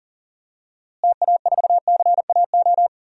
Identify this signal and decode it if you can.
Morse code, heard as a keyed tone, about 40 wpm, 700 Hz. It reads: TA4CAO
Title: DXCC-100 Call Signs - 6 Characters - Repeated 2x - 40wpm